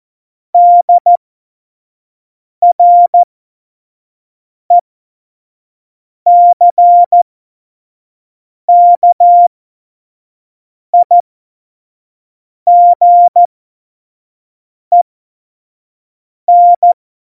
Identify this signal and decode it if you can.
Morse code, heard as a keyed tone, about 14 wpm, spaced as for 5 wpm, 700 Hz. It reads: DRECKIGEN
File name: Morsekurs / Witze / Witz 24.Dezember 2019, 25 F70